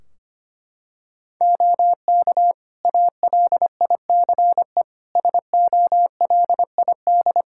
Morse code, heard as a keyed tone, about 25 wpm, 700 Hz. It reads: OK ALICE SOLID
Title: Practice QSO - Exchange 3 - More Info - 25wpm